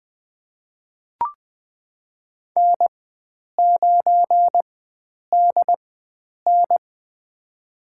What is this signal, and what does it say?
Morse code, heard as a keyed tone, about 21 wpm, spaced as for 9 wpm, 700 Hz. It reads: N9DN
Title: USA Call Signs - 4 Characters - 4x Character Spacing - 20wpm